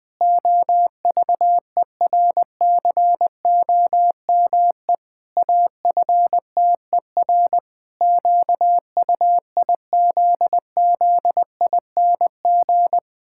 Morse code, heard as a keyed tone, about 19 wpm, 700 Hz. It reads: OVERCOME AFTER QUIZZING
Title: Pangram Challenge v2 - 20wpm